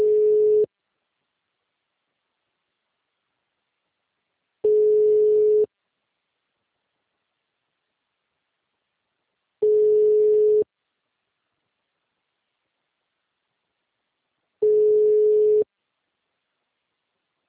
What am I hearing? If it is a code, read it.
Ringback tone (Europe, 425 Hz).
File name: out- -2025- - - .52606